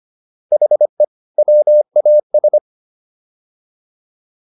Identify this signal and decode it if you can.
Morse code, heard as a keyed tone, about 25 wpm, 600 Hz. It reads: HE WAS
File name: WD104 25